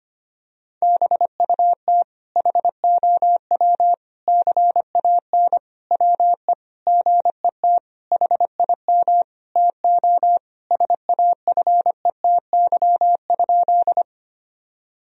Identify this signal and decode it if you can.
Morse code, heard as a keyed tone, about 25 wpm, 700 Hz. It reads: BUT HOW CAN WE GET HIM TO SAFETY?
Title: Sentences from Top 1000 Words - Review Part 4 - 25wpm